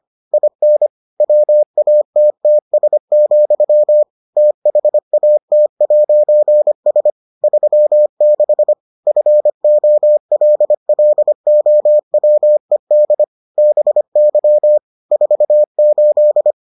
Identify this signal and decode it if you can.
Morse code, heard as a keyed tone, about 25 wpm, 600 Hz. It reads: IN WATTS, THAT'S 36 FOLLOWED BY 48